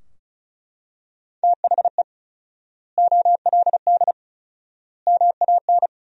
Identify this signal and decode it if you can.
Morse code, heard as a keyed tone, about 36 wpm, 700 Hz. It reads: THE OLD MAN
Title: 4x Word-Spacing - Sentences from Top 500 Words - 35wpm